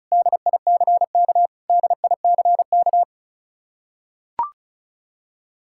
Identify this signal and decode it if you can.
Morse code, heard as a keyed tone, about 34 wpm, 700 Hz. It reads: DICK DICK
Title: Practice QSO - Exchange 1 - Beginning - 35wpm